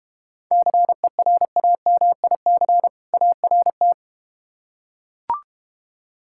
Morse code, heard as a keyed tone, about 33 wpm, 700 Hz. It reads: CERAMIC ART